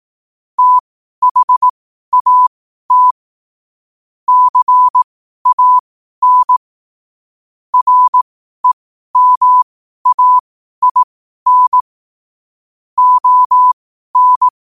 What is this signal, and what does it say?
Morse code, heard as a keyed tone, about 18 wpm, spaced as for 12 wpm, 1000 Hz. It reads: THAT CAN REMAIN ON
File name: qotdfiles / 12wpm